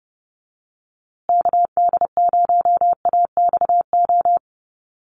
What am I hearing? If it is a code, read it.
Morse code, heard as a keyed tone, about 30 wpm, 700 Hz. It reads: KD0AXO
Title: USA Call Signs Repeated 2x 30wpm